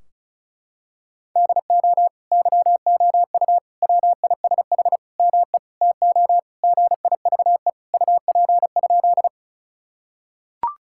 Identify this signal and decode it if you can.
Morse code, heard as a keyed tone, about 34 wpm, 700 Hz. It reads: DO YOU WISH ME TO GIVE UP?
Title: Sentences from Top 1600 Words 35wpm